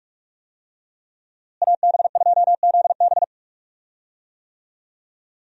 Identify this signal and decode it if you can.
Morse code, heard as a keyed tone, about 44 wpm, 700 Hz. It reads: AB2ZB
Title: USA Call Signs - 5 Characters - Repeated 3x - 45wpm